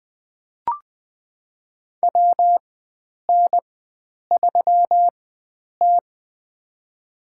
Morse code, heard as a keyed tone, about 20 wpm, spaced as for 9 wpm, 700 Hz. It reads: WN3T
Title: USA Call Signs - 4 Characters - 4x Character Spacing - 20wpm